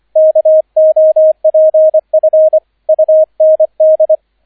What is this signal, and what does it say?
Morse code, heard as a keyed tone, about 25 wpm, 630 Hz, with radio noise. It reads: KOPF UND